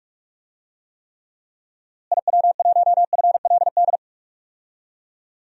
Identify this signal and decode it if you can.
Morse code, heard as a keyed tone, about 46 wpm, 700 Hz. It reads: IW1FLD
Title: DXCC-100 Call Signs - 6 Characters - Repeated 3x - 45wpm